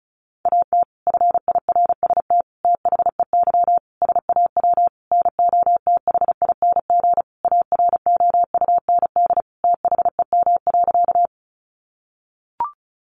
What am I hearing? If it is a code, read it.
Morse code, heard as a keyed tone, about 34 wpm, 700 Hz. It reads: AT FIRST THEY SAW NOTHING AROUND THEM.
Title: Sentences from Top 600 Words 35wpm